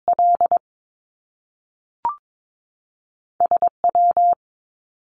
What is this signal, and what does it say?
Morse code, heard as a keyed tone, about 22 wpm, spaced as for 3 wpm, 700 Hz. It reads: LS W
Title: Top Two Letter Combinations in Words 22wpm